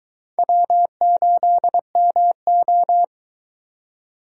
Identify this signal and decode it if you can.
Morse code, heard as a keyed tone, about 22 wpm, 700 Hz. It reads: W8MO